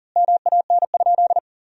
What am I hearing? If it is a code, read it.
Morse code, heard as a keyed tone, about 40 wpm, 700 Hz. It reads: MAN?
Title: Sentences from Top 1000 Words - Review Part 2 - 40wpm